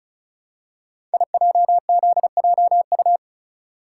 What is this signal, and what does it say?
Morse code, heard as a keyed tone, about 36 wpm, 700 Hz. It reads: IJZJU
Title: Code Groups - 5 Letter:Numbers - 35wpm